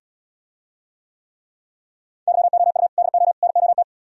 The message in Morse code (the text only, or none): YOU DO NOT